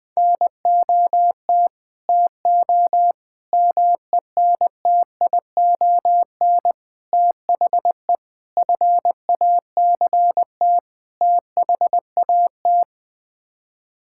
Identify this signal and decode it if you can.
Morse code, heard as a keyed tone, about 20 wpm, 700 Hz. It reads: NOT TO MENTION THE FACT THAT
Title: Sets of 6 Words - 20wpm